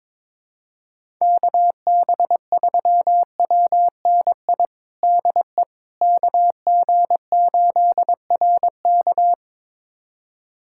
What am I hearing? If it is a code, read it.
Morse code, heard as a keyed tone, about 22 wpm, 700 Hz. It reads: KB3WNI DE KG8RK
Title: USA Callsign DE Callsign 22wpm